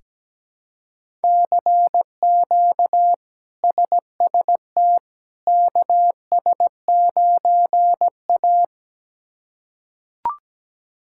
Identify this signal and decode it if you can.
Morse code, heard as a keyed tone, about 17 wpm, 700 Hz. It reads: CQ SST KS9A
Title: K1USN SST QSO Practice - 17wpm